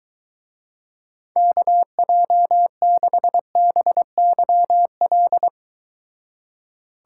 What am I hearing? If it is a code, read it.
Morse code, heard as a keyed tone, about 23 wpm, 700 Hz. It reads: KJ6BYL